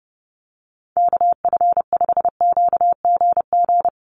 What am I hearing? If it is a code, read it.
Morse code, heard as a keyed tone, about 30 wpm, 700 Hz. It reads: KF5QGG
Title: USA Call Signs Repeated 2x 30wpm